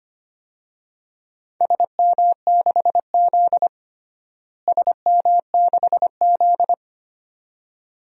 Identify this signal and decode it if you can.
Morse code, heard as a keyed tone, about 25 wpm, 700 Hz. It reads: SM6Z SM6Z
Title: DXCC-100 Call Signs - 4 Characters - Repeated 2x - 25wpm